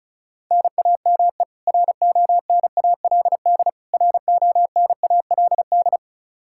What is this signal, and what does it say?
Morse code, heard as a keyed tone, about 34 wpm, 700 Hz. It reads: NAME RONALD RONALD
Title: Practice QSO - Exchange 1 - Beginning - 35wpm